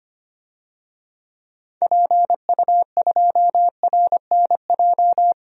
Morse code, heard as a keyed tone, about 25 wpm, 700 Hz. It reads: PU2RNJ
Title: DXCC-100 Call Signs - 6 Characters - 25wpm